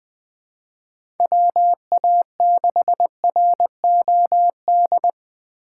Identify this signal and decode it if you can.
Morse code, heard as a keyed tone, about 20 wpm, 700 Hz. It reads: WA6ROD
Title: USA Call Signs - 2x Character Spacing - 20wpm